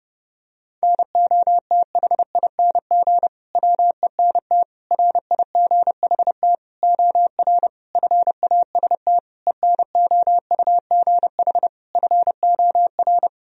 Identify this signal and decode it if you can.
Morse code, heard as a keyed tone, about 30 wpm, 700 Hz. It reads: NOTHING WENT RIGHT OR FAST ENOUGH FOR
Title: Sentences from Top 1000 Words - Review Part 2 - 30wpm